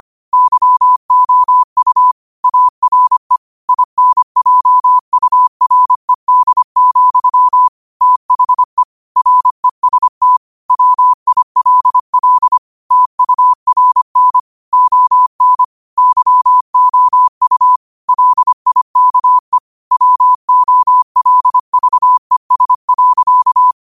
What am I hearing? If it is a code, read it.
Morse code, heard as a keyed tone, about 25 wpm, 1000 Hz. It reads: YOU ARE INJURED, THE REST WILL TURN ON YOU LIKE WOLVES.